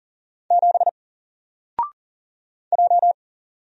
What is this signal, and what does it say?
Morse code, heard as a keyed tone, about 40 wpm, 700 Hz. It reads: 7 J
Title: Single Letter-Number 40wpm